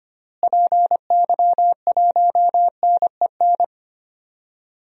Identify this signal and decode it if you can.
Morse code, heard as a keyed tone, about 25 wpm, 700 Hz. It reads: PY1NEN
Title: DXCC-100 Call Signs - 6 Characters - Repeated 3x - 25wpm